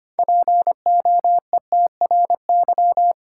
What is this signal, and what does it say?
Morse code, heard as a keyed tone, about 25 wpm, 700 Hz. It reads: POETRY
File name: practice / 25wpm / Difficult Words with Hints - 25wpm